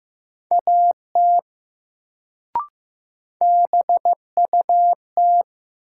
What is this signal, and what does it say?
Morse code, heard as a keyed tone, about 15 wpm, 700 Hz. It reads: AT BUT
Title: Sentences from Top 100 Words - New Word Drill - 15wpm